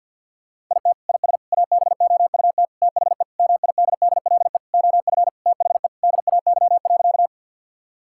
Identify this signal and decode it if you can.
Morse code, heard as a keyed tone, about 50 wpm, 700 Hz. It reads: IT IS ABOUT THE MIDDLE OF THE DAY.
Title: Sentences from Top 1000 Words - Review Part 6 - 50wpm